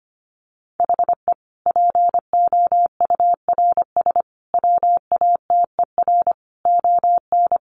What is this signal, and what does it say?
Morse code, heard as a keyed tone, about 24 wpm, 700 Hz. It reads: HE POURS WATER ON
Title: Additional Easy Phrases 25wpm